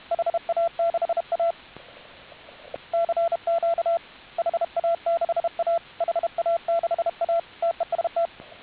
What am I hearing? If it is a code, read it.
Morse code, heard as a keyed tone, about 32 wpm, 680 Hz, with radio noise. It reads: HA6A CQ HA6A HA6A TEST